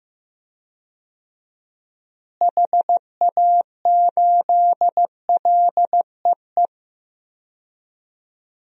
Morse code, heard as a keyed tone, about 16 wpm, 700 Hz. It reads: HA8LEE
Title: DXCC-100 Call Signs - 6 Characters - Repeated 2x - 15wpm